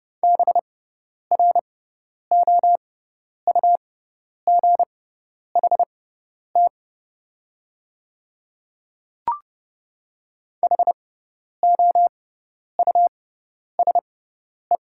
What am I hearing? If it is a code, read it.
Morse code, heard as a keyed tone, about 31 wpm, spaced as for 10 wpm, 690 Hz. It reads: BROUGHT HOUSE
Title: 6x Character Spacing - Top 500 Words - 30wpm